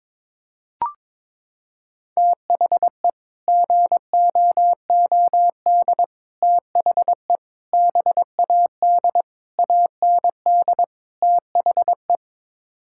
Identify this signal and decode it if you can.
Morse code, heard as a keyed tone, about 22 wpm, 700 Hz. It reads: THE GOOD THE BAD AND THE
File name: practice / 22wpm / Sets of 6 Words - 22wpm